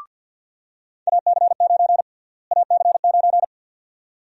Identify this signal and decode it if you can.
Morse code, heard as a keyed tone, about 53 wpm, 700 Hz. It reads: AC9 AC9
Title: USA Call Sign Prefixes - Repeated 2x - 50wpm